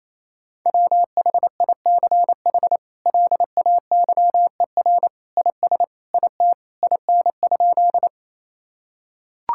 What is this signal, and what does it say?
Morse code, heard as a keyed tone, about 28 wpm, 700 Hz. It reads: WHICH LAYER IS IT IN?
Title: Sentences from Top 1600 Words 28wpm